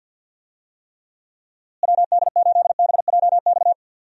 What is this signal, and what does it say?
Morse code, heard as a keyed tone, about 47 wpm, 700 Hz. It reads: WD8BJX
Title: USA Call Signs - 6 Characters - Repeated 3x - 50wpm